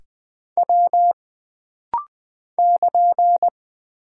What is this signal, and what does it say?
Morse code, heard as a keyed tone, about 20 wpm, 700 Hz. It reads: W <KN>
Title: Single Character (Letter:Number:Punctuation:Prosign) - Rapid-Fire - 20wpm